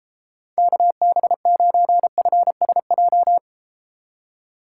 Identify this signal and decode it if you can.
Morse code, heard as a keyed tone, about 33 wpm, 700 Hz. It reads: KB9FSJ